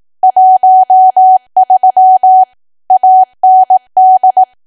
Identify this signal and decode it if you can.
Morse code, heard as a keyed tone, about 18 wpm, 750 Hz. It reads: 13 AND
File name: cwa / W1AW-Level2 / 18WPM